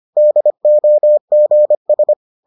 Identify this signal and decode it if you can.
Morse code, heard as a keyed tone, about 25 wpm, 600 Hz. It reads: DOGS